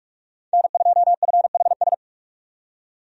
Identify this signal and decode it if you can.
Morse code, heard as a keyed tone, about 46 wpm, 700 Hz. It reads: N2FHS
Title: USA Call Signs - 5 Characters - 45wpm